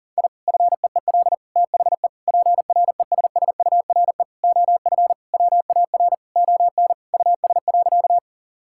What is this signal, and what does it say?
Morse code, heard as a keyed tone, about 37 wpm, 700 Hz. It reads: I FEEL THE PRESSURE OF WAR ON US.
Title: Sentences from Top 1000 Words - Review Part 6 - 40wpm